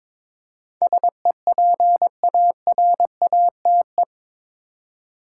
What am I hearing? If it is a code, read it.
Morse code, heard as a keyed tone, about 22 wpm, 700 Hz. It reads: SEPARATE
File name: practice / 22wpm / Eight Letter Words - 22wpm